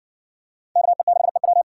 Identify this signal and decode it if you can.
Morse code, heard as a keyed tone, about 57 wpm, 700 Hz. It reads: KE6EW